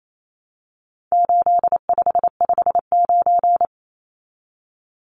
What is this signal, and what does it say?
Morse code, heard as a keyed tone, about 28 wpm, 700 Hz. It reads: 8559
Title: Numbers - 4-Digit 28wpm